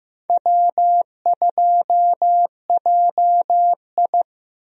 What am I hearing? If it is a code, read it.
Morse code, heard as a keyed tone, about 15 wpm, 700 Hz. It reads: W2JI
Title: USA Call Signs - 4 Characters - Repeated 2x - 15wpm